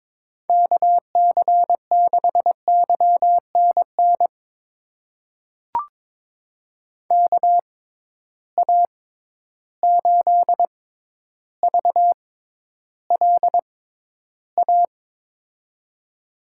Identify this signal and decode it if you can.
Morse code, heard as a keyed tone, about 22 wpm, spaced as for 11 wpm, 700 Hz. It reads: KC6YNN KA8VLA
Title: USA Call Signs - 6 Characters - 6x Character Spacing - 22wpm